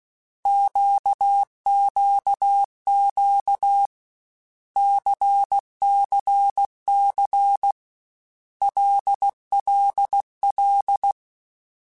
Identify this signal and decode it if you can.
Morse code, heard as a keyed tone, about 16 wpm, 800 Hz. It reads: QQQ CCC LLL
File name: Basic Letter Groups By7Groups / LFCYQ / LFCYQ@13